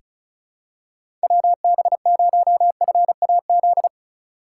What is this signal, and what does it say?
Morse code, heard as a keyed tone, about 36 wpm, 700 Hz. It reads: WB0FAZ
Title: USA Call Signs - 6 Characters - 4x Character Spacing - 35wpm